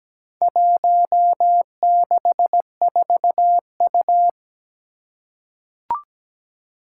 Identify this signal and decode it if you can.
Morse code, heard as a keyed tone, about 17 wpm, 700 Hz. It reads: 164U
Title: Four Challenging Letter-Number Combinations 17wpm